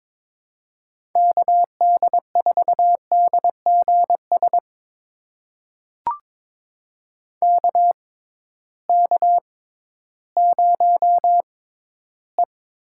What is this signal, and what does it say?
Morse code, heard as a keyed tone, about 22 wpm, spaced as for 13 wpm, 700 Hz. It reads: KD4DGS KK0E
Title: USA Call Signs - 6x Character Spacing - 22wpm